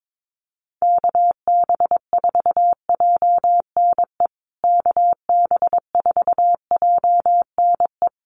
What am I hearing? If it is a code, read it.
Morse code, heard as a keyed tone, about 22 wpm, 700 Hz. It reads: KB4JNE KB4JNE
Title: USA Call Signs Repeated 3x 22wpm